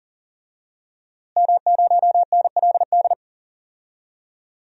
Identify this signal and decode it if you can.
Morse code, heard as a keyed tone, about 40 wpm, 700 Hz. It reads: M0NLD
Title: DXCC-100 Call Signs - 5 Characters - Repeated 2x - 40wpm